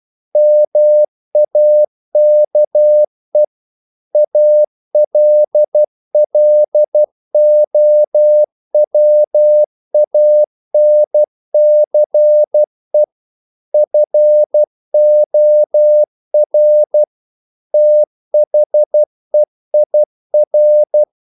Morse code, heard as a keyed tone, about 12 wpm, 600 Hz. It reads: MAKE ALLOWANCE FOR THEIR
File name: CW / Practice / If12